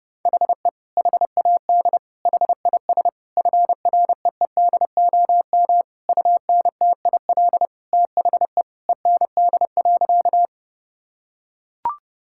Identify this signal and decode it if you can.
Morse code, heard as a keyed tone, about 30 wpm, 700 Hz. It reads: HE HAD HIS FREEDOM UNTIL THE END.
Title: Sentences from Top 1000 Words - Review Part 5 - 30wpm